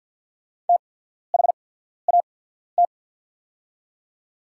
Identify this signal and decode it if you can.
Morse code, heard as a keyed tone, about 48 wpm, spaced as for 14 wpm, 690 Hz. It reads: THAT
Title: 8x Character Spacing - Top 300 Words - 50wpm